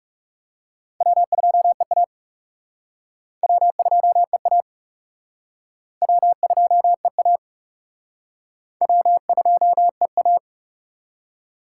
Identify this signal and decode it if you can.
Morse code, heard as a keyed tone, about 36 wpm, 700 Hz. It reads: W2EA W2EA W2EA W2EA
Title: Speed Racing - USA 1x2 Call Signs 30wpm